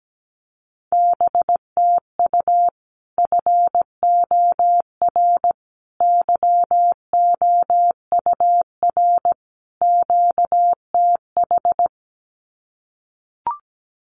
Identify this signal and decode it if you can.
Morse code, heard as a keyed tone, about 17 wpm, 700 Hz. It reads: BTU FOR YOUR QTH
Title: Common QSO Phrases 17wpm